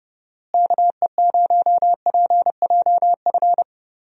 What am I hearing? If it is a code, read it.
Morse code, heard as a keyed tone, about 30 wpm, 700 Hz. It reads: KE0PJF
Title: USA Call Signs - 6 Characters - Repeated 3x - 30wpm